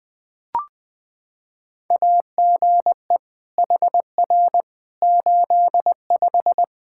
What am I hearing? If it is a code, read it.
Morse code, heard as a keyed tone, about 21 wpm, 700 Hz. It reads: AGE HR 85
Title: Practice QSO - Exchange 3 - More Info - 20wpm